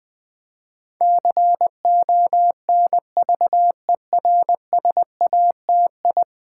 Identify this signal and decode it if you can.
Morse code, heard as a keyed tone, about 20 wpm, 700 Hz. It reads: CONVERSATI
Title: Build A Word - 5 Letters At A Time - 12-Letter Words - 20wpm